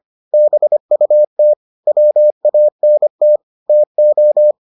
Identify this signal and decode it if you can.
Morse code, heard as a keyed tone, about 25 wpm, 600 Hz. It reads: BUT WANT TO